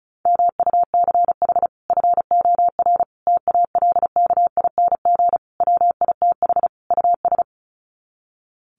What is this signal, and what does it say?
Morse code, heard as a keyed tone, about 34 wpm, 700 Hz. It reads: MUCH FOR TALKING WITH US
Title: Sets of 5 Words Encore 35wpm